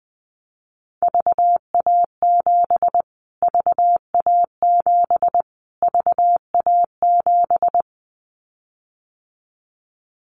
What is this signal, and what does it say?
Morse code, heard as a keyed tone, about 20 wpm, 700 Hz. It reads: VA7 VA7 VA7
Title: Callsign Prefixes for Top 25 DXCC - Repeated 3x - 20wpm